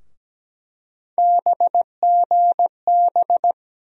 Morse code, heard as a keyed tone, about 17 wpm, 700 Hz. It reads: BGB
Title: USA Call Sign Postfixes - 3 Letters 17wpm